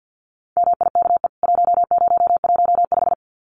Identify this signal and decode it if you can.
Morse code, heard as a keyed tone, about 50 wpm, 700 Hz. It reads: MIKE 1015
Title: CWT Exchange 50wpm